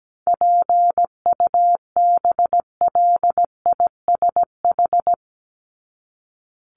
Morse code, heard as a keyed tone, about 17 wpm, 700 Hz. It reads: PUBLISH
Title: ED Suffix - Top 100 Words with Base Words 17wpm